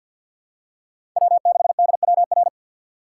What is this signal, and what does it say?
Morse code, heard as a keyed tone, about 47 wpm, 700 Hz. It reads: W6DWR
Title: USA Call Signs - 5 Characters - 4x Character Spacing - 50wpm